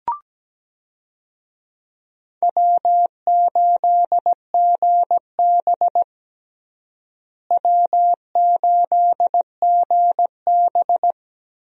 Morse code, heard as a keyed tone, about 17 wpm, 700 Hz. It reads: W8GB W8GB
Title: USA Call Signs - 4 Characters - Repeated 2x - 17wpm